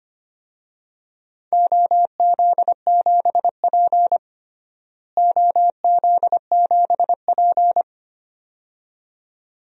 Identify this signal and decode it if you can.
Morse code, heard as a keyed tone, about 25 wpm, 700 Hz. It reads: OZ7P OZ7P
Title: DXCC-100 Call Signs - 4 Characters - Repeated 2x - 25wpm